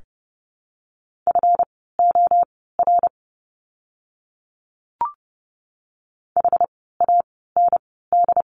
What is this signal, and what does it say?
Morse code, heard as a keyed tone, about 30 wpm, spaced as for 17 wpm, 700 Hz. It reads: FOR HAND
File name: course / 30:10wpm / 237 Lesson 38f - Letter q - Words 30wpm